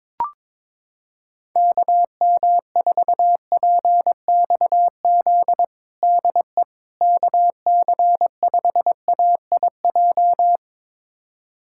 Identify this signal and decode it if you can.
Morse code, heard as a keyed tone, about 22 wpm, 700 Hz. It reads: KM4PXZ DE KC5AIJ
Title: USA Callsign DE Callsign - 6 Characters - 22wpm